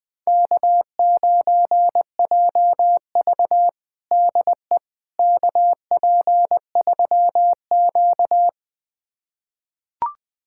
Morse code, heard as a keyed tone, about 20 wpm, 700 Hz. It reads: K9JV DE KP3Q
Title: USA Callsign DE Callsign - 4 Characters - 20wpm